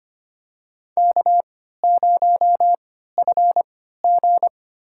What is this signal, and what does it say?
Morse code, heard as a keyed tone, about 25 wpm, spaced as for 14 wpm, 700 Hz. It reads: K0FG
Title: USA Call Signs - 4 Characters - 3x Character Spacing - 25wpm